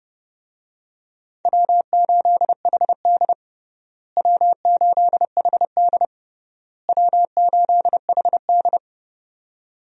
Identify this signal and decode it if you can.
Morse code, heard as a keyed tone, about 32 wpm, 700 Hz. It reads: W8HD W8HD W8HD
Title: USA Call Signs - 4 Characters - Repeated 3x - 30wpm